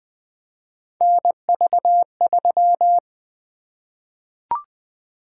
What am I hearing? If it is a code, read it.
Morse code, heard as a keyed tone, about 20 wpm, 700 Hz. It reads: NV3
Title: Three Letter-Number Combinations - 20wpm